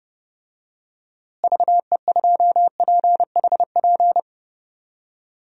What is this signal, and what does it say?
Morse code, heard as a keyed tone, about 31 wpm, 700 Hz. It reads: VE2PHP
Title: DXCC-100 Call Signs - 6 Characters - Repeated 2x - 30wpm